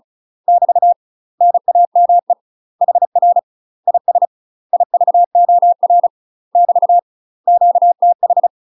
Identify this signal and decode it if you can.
Morse code, heard as a keyed tone, about 34 wpm, 700 Hz. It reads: <BT> NAME HR IS IVOR <BT> QTH
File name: UK MASTER QSOs TRAINING / 35WPM / UK MASTER QSOs 0039